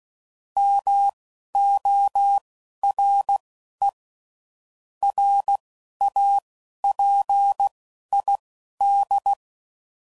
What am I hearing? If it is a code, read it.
Morse code, heard as a keyed tone, about 16 wpm, spaced as for 11 wpm, 800 Hz. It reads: MORE RAPID
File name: CW Code Groups @10WPM / Technical Subjects / XL